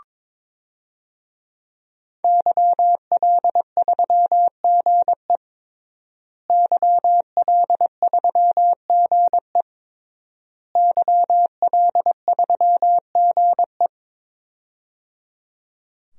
Morse code, heard as a keyed tone, about 22 wpm, 700 Hz. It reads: YL3GE YL3GE YL3GE
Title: DXCC-100 Call Signs - 5 Characters - Repeated 3x - 22wpm